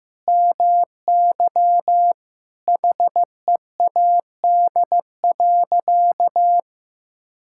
Morse code, heard as a keyed tone, about 15 wpm, 700 Hz. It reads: MY HEAD.
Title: Sentences from Top 1600 Words 15wpm